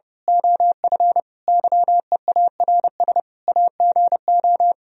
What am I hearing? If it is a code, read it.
Morse code, heard as a keyed tone, about 28 wpm, 700 Hz. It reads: OF YEARS AGO
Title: Speed-Racing - Sets of 5 Words 30wpm